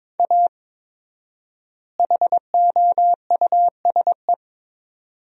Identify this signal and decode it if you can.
Morse code, heard as a keyed tone, about 22 wpm, 700 Hz. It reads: A HOUSE
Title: 4x Word-Spacing - Sentences from Top 400 Words - 22wpm